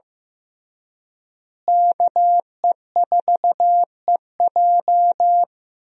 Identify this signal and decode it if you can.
Morse code, heard as a keyed tone, about 15 wpm, 700 Hz. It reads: KE4EJ
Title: USA Call Signs - 5 Characters - Repeated 3x - 15wpm